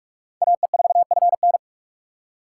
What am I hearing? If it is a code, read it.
Morse code, heard as a keyed tone, about 44 wpm, 700 Hz. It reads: AE4FN